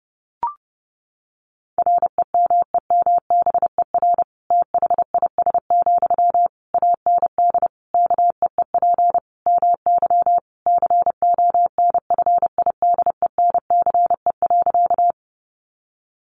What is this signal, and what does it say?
Morse code, heard as a keyed tone, about 30 wpm, 700 Hz. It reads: REMEMBER THIS, AND KEEP MY CONFIDENCE.
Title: Sentences from Top 1500 Words 30wpm